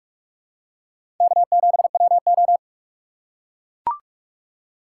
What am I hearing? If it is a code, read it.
Morse code, heard as a keyed tone, about 44 wpm, 700 Hz. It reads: K7WO
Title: USA Call Signs - 6x Character Spacing - 45wpm